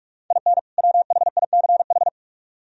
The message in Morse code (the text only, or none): IN WHICH